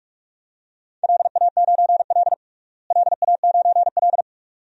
Morse code, heard as a keyed tone, about 46 wpm, 700 Hz. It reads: LA9L LA9L
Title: DXCC-100 Call Signs - 4 Characters - Repeated 2x - 45wpm